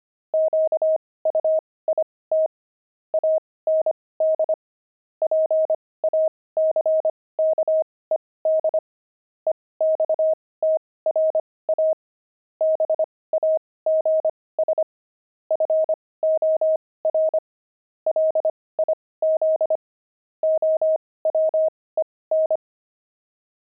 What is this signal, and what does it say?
Morse code, heard as a keyed tone, about 25 wpm, spaced as for 18 wpm, 630 Hz. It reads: QUIT AND PACKED EXTRA BAGS FOR LIZ OWEN
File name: pangram-fs-18-cs-25-quit